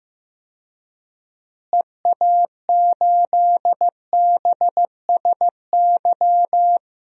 Morse code, heard as a keyed tone, about 15 wpm, 700 Hz. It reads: EA8BSY
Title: DXCC-100 Call Signs - 6 Characters - Repeated 2x - 15wpm